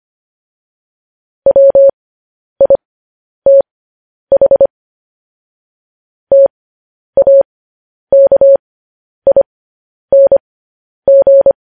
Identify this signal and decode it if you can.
Morse code, heard as a keyed tone, about 25 wpm, spaced as for 10 wpm, 550 Hz. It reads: WITH TAKING
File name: podcast / News headlines-1x-25x10